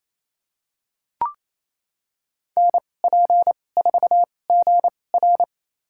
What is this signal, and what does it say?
Morse code, heard as a keyed tone, about 28 wpm, spaced as for 20 wpm, 700 Hz. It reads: NP4GR
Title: USA Call Signs - 2x Character Spacing - 28wpm